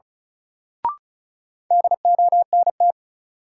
Morse code, heard as a keyed tone, about 36 wpm, 700 Hz. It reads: DONT
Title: Sentences from Top 300 Words - New Word Drill - 35wpm